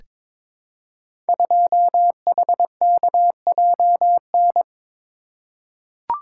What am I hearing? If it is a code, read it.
Morse code, heard as a keyed tone, about 23 wpm, 700 Hz. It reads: 2HKJN
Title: Code Groups - 5 Letter:Numbers - 22wpm